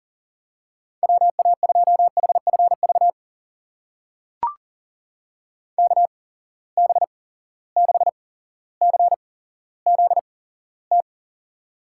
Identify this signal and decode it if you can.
Morse code, heard as a keyed tone, about 40 wpm, spaced as for 17 wpm, 700 Hz. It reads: WA2HFV KB6CZT
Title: USA Call Signs - 6 Characters - 8x Character Spacing - 40wpm